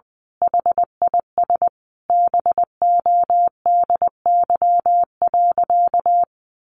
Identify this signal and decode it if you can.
Morse code, heard as a keyed tone, about 19 wpm, 700 Hz. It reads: HIS BODY.
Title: A Sound Of Thunder 20wpm